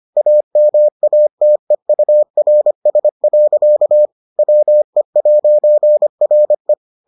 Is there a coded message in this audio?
Yes: Morse code from 0.2 to 6.7 s, heard as a keyed tone, about 25 wpm, 600 Hz. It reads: AMATEURS. WE'RE